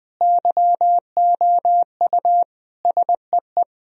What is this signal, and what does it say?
Morse code, heard as a keyed tone, about 20 wpm, 700 Hz. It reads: YOU SEE
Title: Sentences from Top 1000 Words - Review Part 2 - 20wpm